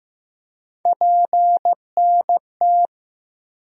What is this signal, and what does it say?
Morse code, heard as a keyed tone, about 15 wpm, 700 Hz. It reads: PNT
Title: USA Call Sign Postfixes - 3 Letters 15wpm